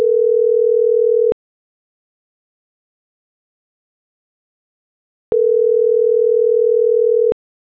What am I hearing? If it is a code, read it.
Ringback tone (North America), 440 + 480 Hz.